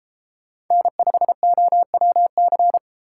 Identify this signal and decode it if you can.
Morse code, heard as a keyed tone, about 33 wpm, 700 Hz. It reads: N5OWC